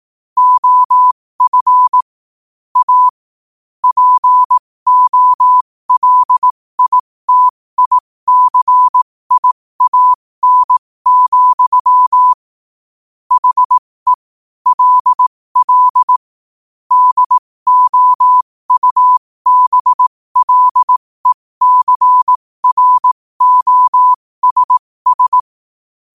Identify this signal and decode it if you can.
Morse code, heard as a keyed tone, about 18 wpm, spaced as for 15 wpm, 1000 Hz. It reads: OF A POLITICIAN, HE LL DOUBLECROSS
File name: qotdfiles / 15wpm